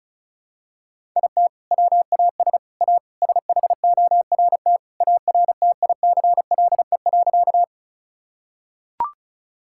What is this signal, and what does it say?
Morse code, heard as a keyed tone, about 36 wpm, 700 Hz. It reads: IT WAS A SHORT ARTICLE.
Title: Sentences from Top 1000 Words - Review Part 5 - 35wpm